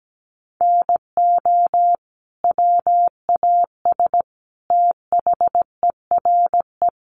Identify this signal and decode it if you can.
Morse code, heard as a keyed tone, about 17 wpm, 700 Hz. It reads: NO WAS THERE
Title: Sentences from Top 100 Words 17wpm